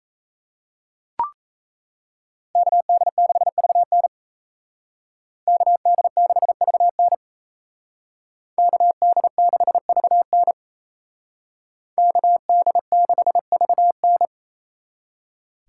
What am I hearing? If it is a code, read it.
Morse code, heard as a keyed tone, about 36 wpm, 700 Hz. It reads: KD6VN KD6VN KD6VN KD6VN